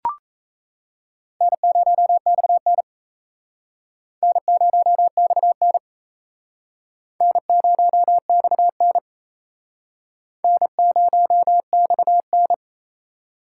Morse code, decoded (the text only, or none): N0XN N0XN N0XN N0XN